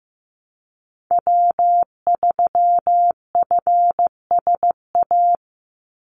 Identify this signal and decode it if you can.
Morse code, heard as a keyed tone, about 15 wpm, 700 Hz. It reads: W3FSA
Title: Call Signs Repeated 2x - 15wpm